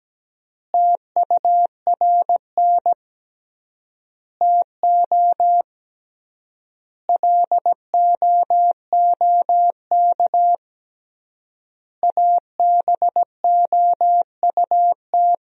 Morse code, heard as a keyed tone, about 17 wpm, 700 Hz. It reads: TURN TO LOOK ABOUT
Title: 3x Word-Spacing - Sentences from Top 200 Words - 17wpm